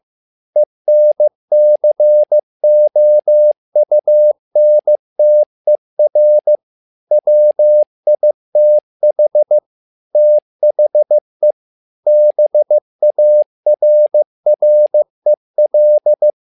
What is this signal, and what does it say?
Morse code, heard as a keyed tone, about 15 wpm, 600 Hz. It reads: ENCOUNTER WITH THE BARREL
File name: learning center / tower trouble / tower trouble 15wpm